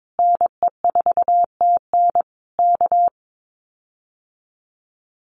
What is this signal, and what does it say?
Morse code, heard as a keyed tone, about 22 wpm, 700 Hz. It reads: NE4TN K
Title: Common QSO Phrases Encore 22wpm